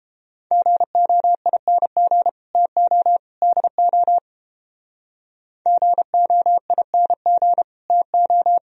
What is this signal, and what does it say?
Morse code, heard as a keyed tone, about 30 wpm, 700 Hz. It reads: GOING TO DO GOING TO